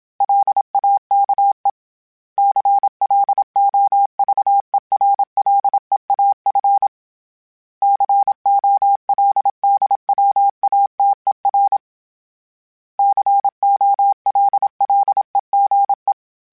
Morse code, heard as a keyed tone, about 27 wpm, 800 Hz. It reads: LAKE CLOVERLEAF COLDWATER COLLEGE
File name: fromSMs / FromSMs mp3s / Cities@25